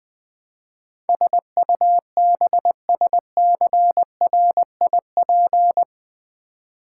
Morse code, heard as a keyed tone, about 20 wpm, 700 Hz. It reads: SUBSCRIP
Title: Build A Word - 4 Letters At A Time - 12-Letter Words - 20wpm